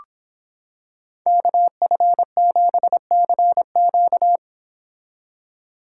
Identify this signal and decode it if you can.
Morse code, heard as a keyed tone, about 26 wpm, 700 Hz. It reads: KF7CQ